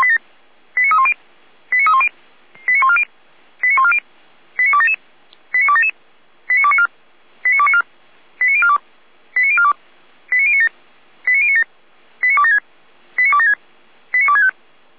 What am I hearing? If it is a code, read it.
Five-tone selective call (ZVEI), heard as quick runs of five tones, tones of about 70 ms. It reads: 89310 89310 89140 89140 89277 89277 89294 89294 89042 89042 89097 89097 89276 89276 89265
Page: /455/SDR/ZVEI/89229/